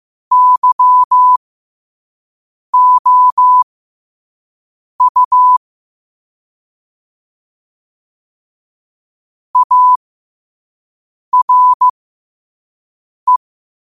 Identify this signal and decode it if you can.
Morse code, heard as a keyed tone, about 15 wpm, spaced as for 5 wpm, 1000 Hz. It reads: YOU ARE
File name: qotdfiles / 5wpm